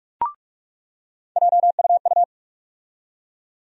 Morse code, heard as a keyed tone, about 46 wpm, 700 Hz. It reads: JUU
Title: USA Call Sign Postfixes - 3 Letters 45wpm